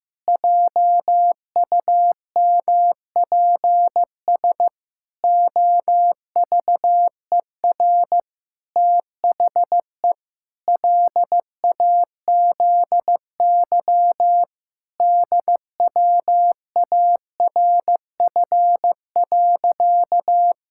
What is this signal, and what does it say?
Morse code, heard as a keyed tone, about 15 wpm, 700 Hz. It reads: JUMPS OVER THE LAZY DWARF.
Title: Pangram Challenge v2 - 15wpm